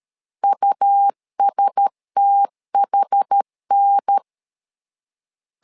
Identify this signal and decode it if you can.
Morse code, heard as a keyed tone, about 13 wpm, 800 Hz. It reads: USTHN
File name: LEZIONI / 5 LBVC / 5 L 12wpm 1913